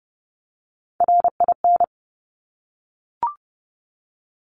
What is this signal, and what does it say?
Morse code, heard as a keyed tone, about 30 wpm, 700 Hz. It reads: RIN
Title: Top Three Letter Combinations in Words 30wpm